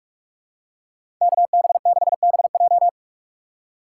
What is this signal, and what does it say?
Morse code, heard as a keyed tone, about 46 wpm, 700 Hz. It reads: KB6BJ
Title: USA Call Signs - 5 Characters - 2x Character Spacing - 45wpm